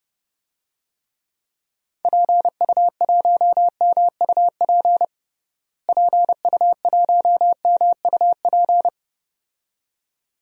Morse code, heard as a keyed tone, about 30 wpm, 700 Hz. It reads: PU1MUP PU1MUP
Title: DXCC-100 Call Signs - 6 Characters - Repeated 2x - 30wpm